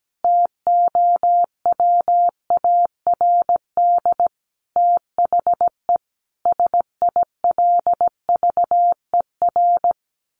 Morse code, heard as a keyed tone, about 17 wpm, 700 Hz. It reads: TOWARD THE SILVER
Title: A Sound Of Thunder 17wpm